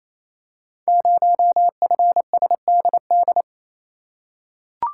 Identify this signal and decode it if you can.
Morse code, heard as a keyed tone, about 29 wpm, 700 Hz. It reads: 0FSDD
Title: Code Groups - 5 Letter:Numbers - 28wpm